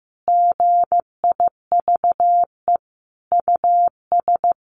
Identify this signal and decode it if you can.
Morse code, heard as a keyed tone, about 15 wpm, 700 Hz. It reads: GIVE US
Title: Sentences from Top 1300 Words 15wpm